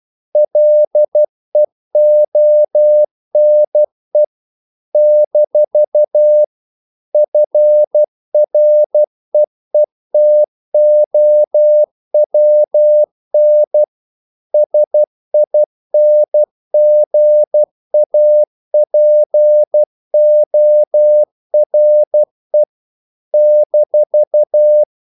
Morse code, heard as a keyed tone, about 12 wpm, 600 Hz. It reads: LEONE - FREETOWN SINGAPORE -
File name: CW / Practice / NationalCapitals12